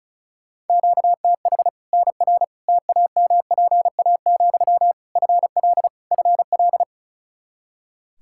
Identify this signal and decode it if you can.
Morse code, heard as a keyed tone, about 36 wpm, 700 Hz. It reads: QTH NR TAMPA, FL FL
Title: K1USN SST QSO Practice 35wpm